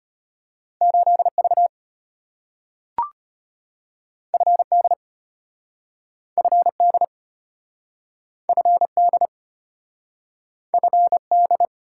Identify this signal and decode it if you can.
Morse code, heard as a keyed tone, about 33 wpm, 700 Hz. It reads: 8V FD FD FD FD